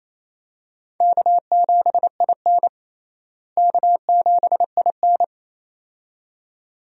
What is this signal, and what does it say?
Morse code, heard as a keyed tone, about 28 wpm, 700 Hz. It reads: K7IN K7IN
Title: USA Call Signs - 4 Characters - Repeated 2x - 28wpm